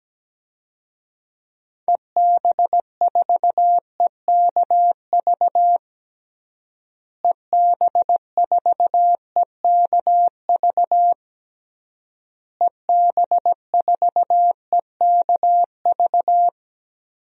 Morse code, heard as a keyed tone, about 17 wpm, 700 Hz. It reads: EB4EKV EB4EKV EB4EKV
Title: DXCC-100 Call Signs - 6 Characters - Repeated 3x - 17wpm